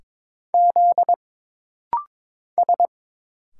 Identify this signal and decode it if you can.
Morse code, heard as a keyed tone, about 22 wpm, 700 Hz. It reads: Z S
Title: Single Letters - Rapid-Fire 22wpm